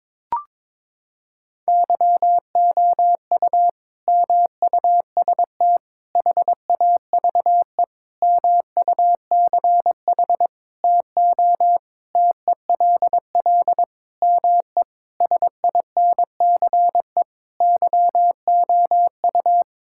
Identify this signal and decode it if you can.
Morse code, heard as a keyed tone, about 22 wpm, 700 Hz. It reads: YOU MUST HAVE MUCH TO TELL ME SINCE YOU
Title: Sentences from Top 1600 Words 22wpm